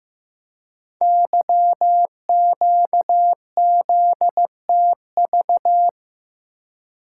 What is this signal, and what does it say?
Morse code, heard as a keyed tone, about 15 wpm, 700 Hz. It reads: YQZTV
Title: Code Groups - 5 Letter:Numbers - 15wpm